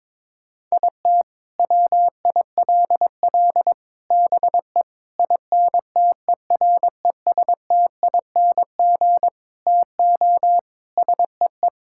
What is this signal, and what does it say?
Morse code, heard as a keyed tone, about 22 wpm, 700 Hz. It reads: IT WILL BE INTERESTING TO SEE
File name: practice / 22wpm / Sets of 8 Words - 22wpm